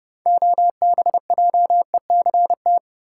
Morse code, heard as a keyed tone, about 30 wpm, 700 Hz. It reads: OBJECT
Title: Sentences from Top 500 Words - New Word Drill - 30wpm